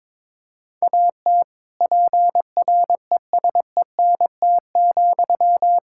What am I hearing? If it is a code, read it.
Morse code, heard as a keyed tone, about 22 wpm, 700 Hz. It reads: AT PRESENT,
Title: Sentences from Top 1000 Words - Review Part 5 - 22wpm